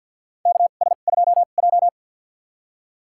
Morse code, heard as a keyed tone, about 47 wpm, spaced as for 35 wpm, 700 Hz. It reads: KS2J
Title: USA Call Signs - 4 Characters - 2x Character Spacing - 50wpm